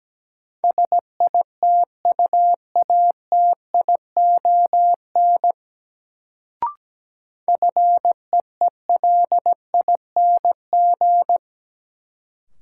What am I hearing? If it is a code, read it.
Morse code, heard as a keyed tone, about 17 wpm, 700 Hz. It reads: SITUATION FEELING
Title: Sentences from Top 1200 Words - New Word Drill Alt - 17wpm